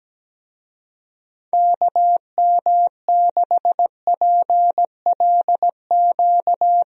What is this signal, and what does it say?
Morse code, heard as a keyed tone, about 17 wpm, 700 Hz. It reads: KM6PLQ
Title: USA Call Signs - 6 Characters - Repeated 3x - 17wpm